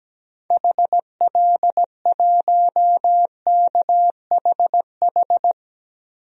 Morse code, heard as a keyed tone, about 17 wpm, 700 Hz. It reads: HL1KHH
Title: DXCC-100 Call Signs - 6 Characters - 17wpm